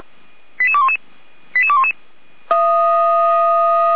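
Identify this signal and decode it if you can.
Five-tone selective call (ZVEI), heard as quick runs of five tones, tones of about 70 ms. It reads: 88211 88211
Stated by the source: ZVEI five-tone selective call received on an SDR.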